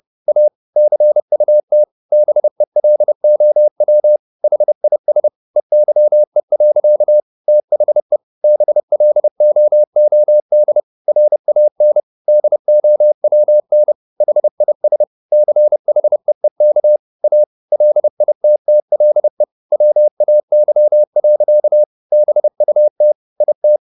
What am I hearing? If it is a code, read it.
Morse code, heard as a keyed tone, about 30 wpm, 600 Hz. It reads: A CUT BELOW HIS EYE. THE BLOOD RAN DOWN HIS CHEEK A LITTLE WAY. BUT IT
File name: the old man and the sea / the old man and the sea 1 30wpm